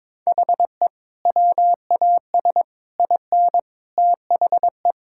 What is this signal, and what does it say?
Morse code, heard as a keyed tone, about 22 wpm, 700 Hz. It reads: HE WAS IN THE
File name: practice / 22wpm / Sentences with 3-Word Phrases - 22wpm